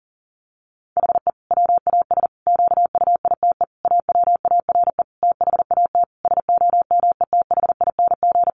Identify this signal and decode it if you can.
Morse code, heard as a keyed tone, about 40 wpm, 700 Hz. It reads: HE WAS QUITE AWARE THAT SOMETHING
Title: Sentences from Top 1400 Words 40wpm